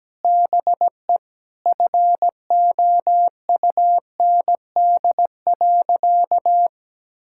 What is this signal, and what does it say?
Morse code, heard as a keyed tone, about 17 wpm, 700 Hz. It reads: BE FOUND.
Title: Sentences from Top 1000 Words - Review Part 5 - 17wpm